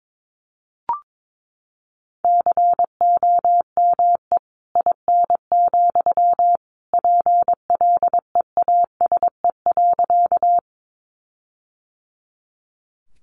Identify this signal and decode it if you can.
Morse code, heard as a keyed tone, about 22 wpm, 700 Hz. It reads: COME IN, PLEASE.
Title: Sentences from Top 1500 Words 22wpm